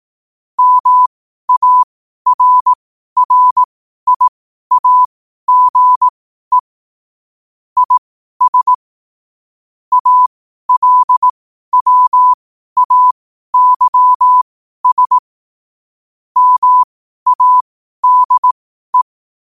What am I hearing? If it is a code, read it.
Morse code, heard as a keyed tone, about 18 wpm, spaced as for 12 wpm, 1000 Hz. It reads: MARRIAGE IS ALWAYS MADE
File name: qotdfiles / 12wpm